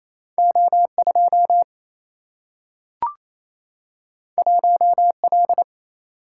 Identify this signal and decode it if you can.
Morse code, heard as a keyed tone, about 28 wpm, 700 Hz. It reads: O2 1L
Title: Single-Letter Single-Number 28wpm